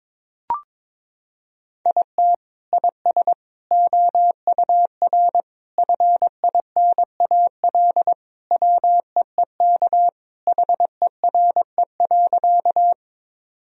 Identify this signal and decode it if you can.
Morse code, heard as a keyed tone, about 22 wpm, 700 Hz. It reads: IT IS OUR FINAL WEEK HERE.
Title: Sentences from Top 1000 Words - Review Part 2 - 22wpm